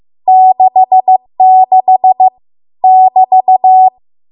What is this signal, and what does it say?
Morse code, heard as a keyed tone, about 15 wpm, 750 Hz. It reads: 66 <BT>